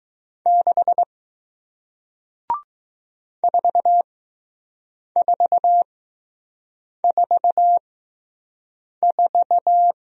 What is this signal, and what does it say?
Morse code, heard as a keyed tone, about 20 wpm, spaced as for 6 wpm, 700 Hz. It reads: 6 4444